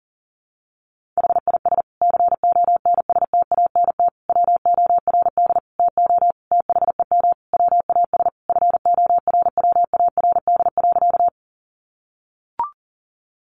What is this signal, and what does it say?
Morse code, heard as a keyed tone, about 40 wpm, 700 Hz. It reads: HIS CONSTANT WORD TO THEM WAS FORWARD.
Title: Sentences from Top 1500 Words 40wpm